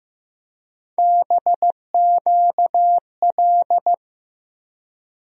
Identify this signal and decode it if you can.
Morse code, heard as a keyed tone, about 15 wpm, 700 Hz. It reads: BQL
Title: USA Call Sign Postfixes - 3 Letters 15wpm